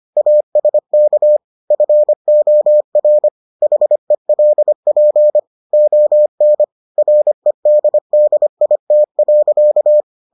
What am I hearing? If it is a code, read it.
Morse code, heard as a keyed tone, about 25 wpm, 600 Hz. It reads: ASK FOR HELP ON REDDIT.